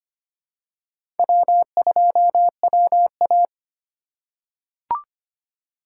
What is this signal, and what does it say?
Morse code, heard as a keyed tone, about 25 wpm, 700 Hz. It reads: W2WA
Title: USA Call Signs - 4 Characters - 3x Character Spacing - 25wpm